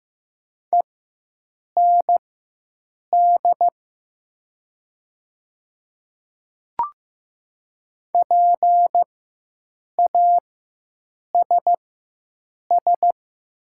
Morse code, heard as a keyed tone, about 15 wpm, spaced as for 7 wpm, 700 Hz. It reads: END PASS
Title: 4x Character Spacing - Top 500 Words - 15wpm